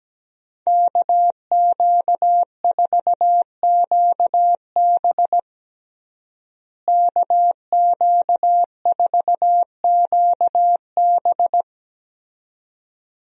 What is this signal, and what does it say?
Morse code, heard as a keyed tone, about 17 wpm, 700 Hz. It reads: KQ4QB KQ4QB
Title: USA Call Signs - 5 Characters - Repeated 2x - 17wpm